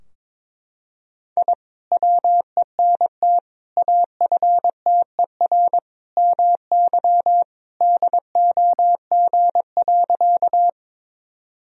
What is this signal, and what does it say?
Morse code, heard as a keyed tone, about 22 wpm, 700 Hz. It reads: I WENT AFTER MY DOG.
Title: Sentences from Top 1000 Words - Review Part 2 - 22wpm